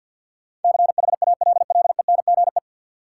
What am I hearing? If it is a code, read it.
Morse code, heard as a keyed tone, about 53 wpm, 700 Hz. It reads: CHALLENGE
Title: Sentences from Top 1000 Words - Review Part 7 - New Word Drill - 50wpm